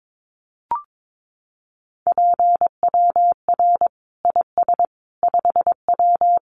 Morse code, heard as a keyed tone, about 22 wpm, 700 Hz. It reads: PWR IS 5W
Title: Common QSO Phrases 22wpm